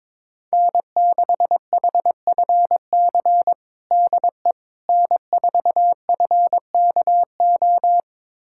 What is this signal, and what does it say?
Morse code, heard as a keyed tone, about 22 wpm, 700 Hz. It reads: N6HFC DE N4FKO
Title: USA Callsign DE Callsign - 5 Characters - 22wpm